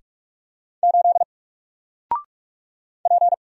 Morse code, heard as a keyed tone, about 44 wpm, 700 Hz. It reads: 8 P
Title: Single Character (Letter:Number:Punctuation:Prosign) Alt 45wpm